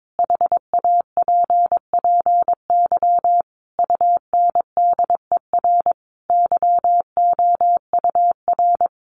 Morse code, heard as a keyed tone, about 22 wpm, 700 Hz. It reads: HAPPY UNDER YOUR
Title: Sentences from Top 1500 Words 22wpm